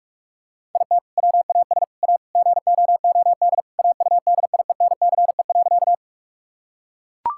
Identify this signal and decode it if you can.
Morse code, heard as a keyed tone, about 41 wpm, 700 Hz. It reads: IT WAS A GOOD AUDIENCE.
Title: Sentences from Top 1000 Words - Review Part 5 - 45wpm